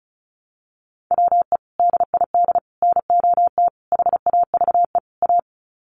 Morse code, heard as a keyed tone, about 34 wpm, 700 Hz. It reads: WE DID NOT HAVE A
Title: Sets of 5 Words Encore2 35wpm